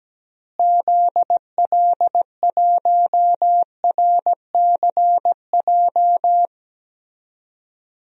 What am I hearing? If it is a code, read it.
Morse code, heard as a keyed tone, about 17 wpm, 700 Hz. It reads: ZL1RCJ
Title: DXCC-100 Call Signs - 6 Characters - Repeated 2x - 17wpm